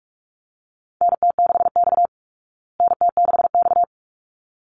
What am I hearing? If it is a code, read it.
Morse code, heard as a keyed tone, about 44 wpm, 700 Hz. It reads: NT6X NT6X
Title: Call Signs Repeated 3x - 45wpm